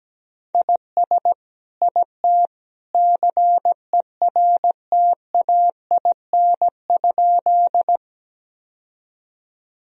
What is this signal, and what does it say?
Morse code, heard as a keyed tone, about 17 wpm, 700 Hz. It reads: IS IT CERTAIN?
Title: Sentences from Top 1000 Words - Review Part 2 - 17wpm